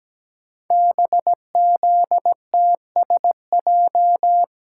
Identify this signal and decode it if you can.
Morse code, heard as a keyed tone, about 17 wpm, 700 Hz. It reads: BZTSJ
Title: Code Groups - 5 Letters - 17wpm